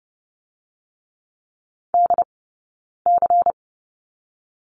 Morse code, heard as a keyed tone, about 30 wpm, 700 Hz. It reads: D C
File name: Lesson 16d - Letter c - Two Characters 30wpm